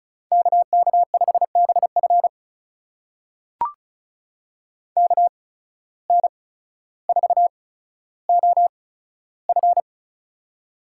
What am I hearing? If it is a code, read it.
Morse code, heard as a keyed tone, about 36 wpm, spaced as for 15 wpm, 700 Hz. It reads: KK5BF KN4OF
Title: USA Call Signs - 5 Characters - 8x Character Spacing - 35wpm